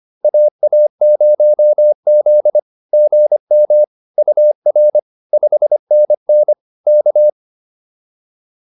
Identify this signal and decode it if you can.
Morse code, heard as a keyed tone, about 25 wpm, 600 Hz. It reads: AA0Z GM UR 5NN K